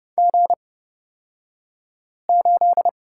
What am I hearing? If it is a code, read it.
Morse code, heard as a keyed tone, about 29 wpm, spaced as for 5 wpm, 700 Hz. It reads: G8
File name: Lesson 37d - Letter x - Two Characters 30wpm